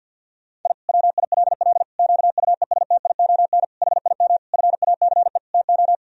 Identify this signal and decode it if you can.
Morse code, heard as a keyed tone, about 50 wpm, 700 Hz. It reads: I WILL QUESTION HIM FACE TO
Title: Sentences from Top 1000 Words - Review Part 2 - 50wpm